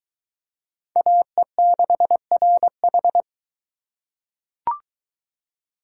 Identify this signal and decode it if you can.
Morse code, heard as a keyed tone, about 21 wpm, 700 Hz. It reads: AE6RH